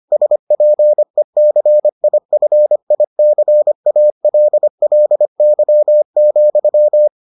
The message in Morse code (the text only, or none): SPECIFICALLY,